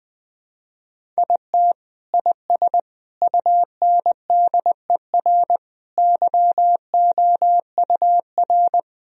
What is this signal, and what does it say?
Morse code, heard as a keyed tone, about 20 wpm, 700 Hz. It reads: IT IS UNDER YOUR
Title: Sentences from Top 1000 Words - Review Part 1 - 20wpm